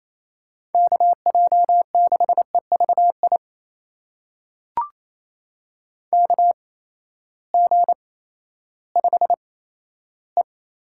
Morse code, heard as a keyed tone, about 28 wpm, spaced as for 14 wpm, 700 Hz. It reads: KJ6EVI KG5E
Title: USA Call Signs - 6 Characters - 8x Character Spacing - 28wpm